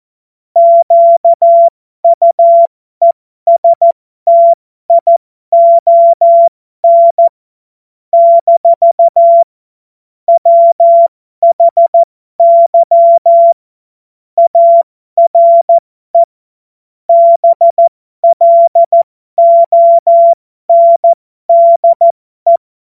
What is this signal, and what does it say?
Morse code, heard as a keyed tone, about 14 wpm, spaced as for 12 wpm, 680 Hz. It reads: QUESTION - WHY ARE BLONDE
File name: Morsekurs / Witze